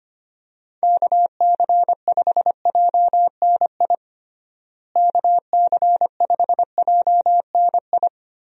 Morse code, heard as a keyed tone, about 25 wpm, 700 Hz. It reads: KC5JNI KC5JNI
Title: USA Call Signs - 6 Characters - Repeated 3x - 25wpm